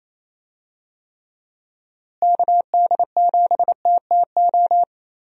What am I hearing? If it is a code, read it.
Morse code, heard as a keyed tone, about 28 wpm, 700 Hz. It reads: KD7TTO
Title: USA Call Signs - 6 Characters - Repeated 3x - 28wpm